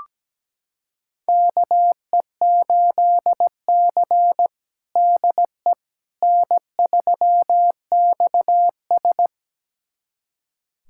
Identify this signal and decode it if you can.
Morse code, heard as a keyed tone, about 17 wpm, 700 Hz. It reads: KE8C DE N3XS
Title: USA Callsign DE Callsign - 4 Characters - 17wpm